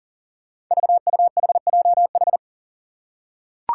Morse code, heard as a keyed tone, about 40 wpm, 700 Hz. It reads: VUHJH
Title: Five Challenging Letter-Number Combinations 40wpm